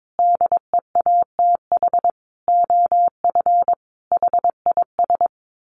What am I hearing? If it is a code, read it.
Morse code, heard as a keyed tone, about 22 wpm, 700 Hz. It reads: DEATH OF HIS
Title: Sentences from Top 1500 Words 22wpm